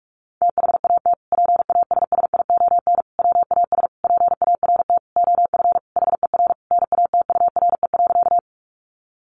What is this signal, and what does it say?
Morse code, heard as a keyed tone, about 44 wpm, 700 Hz. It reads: THAT PASSION WAS PART OF HER NATURE.
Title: Sentences from Top 1300 Words 45wpm